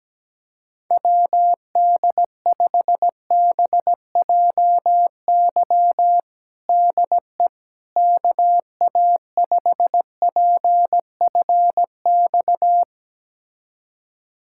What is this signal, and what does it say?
Morse code, heard as a keyed tone, about 17 wpm, 700 Hz. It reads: WD5BJY DE KA5PFX
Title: USA Callsign DE Callsign - 6 Characters - 17wpm